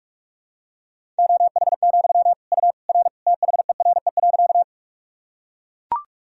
Morse code, heard as a keyed tone, about 41 wpm, 700 Hz. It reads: OH, U R THERE.
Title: Sentences from Top 500 Words and Abbreviations 45wpm